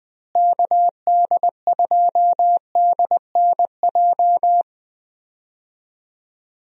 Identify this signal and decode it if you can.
Morse code, heard as a keyed tone, about 20 wpm, 700 Hz. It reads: KD2DNJ
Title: USA Call Signs - 6 Characters - Repeated 2x - 20wpm